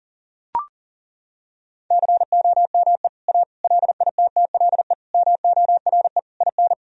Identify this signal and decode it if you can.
Morse code, heard as a keyed tone, about 40 wpm, 700 Hz. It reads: COME A LITTLE MORE IN
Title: Sentences from Top 1000 Words - Review Part 2 - 40wpm